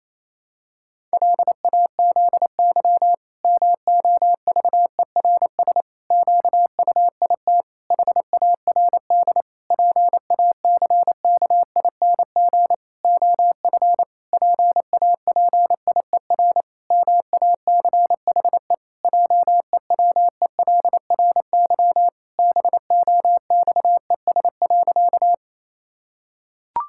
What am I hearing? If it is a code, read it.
Morse code, heard as a keyed tone, about 28 wpm, 700 Hz. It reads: LAZY MOVERS QUIT HARD PACKING OF PAPIER MACHE JEWELRY BOXES.
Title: Pangram Challenge v2 - 28wpm